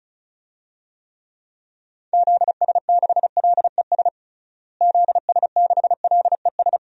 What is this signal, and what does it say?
Morse code, heard as a keyed tone, about 36 wpm, 700 Hz. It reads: ZS6LES ZS6LES
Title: DXCC-100 Call Signs - 6 Characters - Repeated 3x - 35wpm